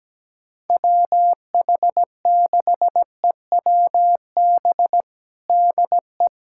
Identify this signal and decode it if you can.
Morse code, heard as a keyed tone, about 17 wpm, 700 Hz. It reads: WH6EWB DE
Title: USA Callsign DE Callsign - 6 Characters - 17wpm